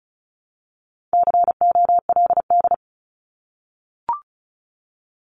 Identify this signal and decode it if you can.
Morse code, heard as a keyed tone, about 34 wpm, 700 Hz. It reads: COLD
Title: Top 500 Words Repeated 35wpm